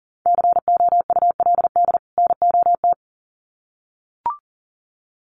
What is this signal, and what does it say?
Morse code, heard as a keyed tone, about 40 wpm, 700 Hz. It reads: COULD NOT
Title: Sets of 2 Words 40wpm